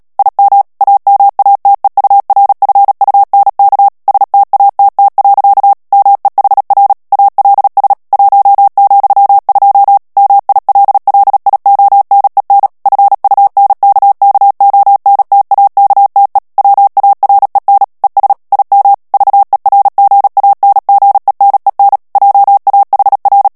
Morse code, heard as a keyed tone, about 37 wpm, 800 Hz. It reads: IM AMATEURFUNK STATT. MEHR ALS 1,2 MILLIONEN FUNKKONTAKTE WAREN ES IM VERGANGENEN JAHR
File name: morse mp3 / ALLEMAND texte clair 2 vitesse 35 mots minute